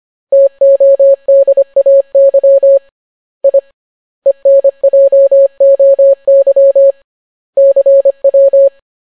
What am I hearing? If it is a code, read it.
Morse code, heard as a keyed tone, about 25 wpm, 550 Hz. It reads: TODAY I ENJOY CW